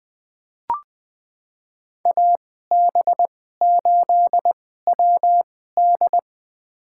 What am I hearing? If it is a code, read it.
Morse code, heard as a keyed tone, about 20 wpm, spaced as for 14 wpm, 700 Hz. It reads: AB8WD
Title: USA Call Signs - 5 Characters - 2x Character Spacing - 20wpm